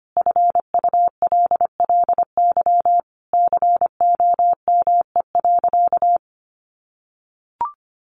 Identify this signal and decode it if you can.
Morse code, heard as a keyed tone, about 25 wpm, 700 Hz. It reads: FULLY COME.
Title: Sentences from Top 1300 Words 25wpm